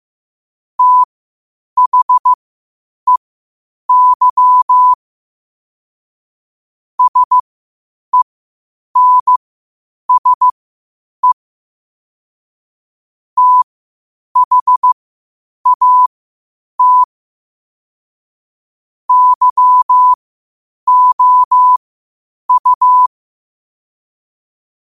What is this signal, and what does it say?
Morse code, heard as a keyed tone, about 15 wpm, spaced as for 8 wpm, 1000 Hz. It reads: THEY SENSE THAT YOU